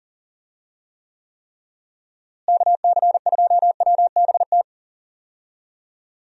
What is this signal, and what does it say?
Morse code, heard as a keyed tone, about 40 wpm, 700 Hz. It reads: KC2WBT
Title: USA Call Signs - 6 Characters - Repeated 3x - 40wpm